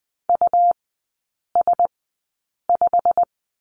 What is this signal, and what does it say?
Morse code, heard as a keyed tone, about 20 wpm, spaced as for 8 wpm, 700 Hz. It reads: US5
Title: 114 Lesson 19e - Letter w - Three Characters 20wpm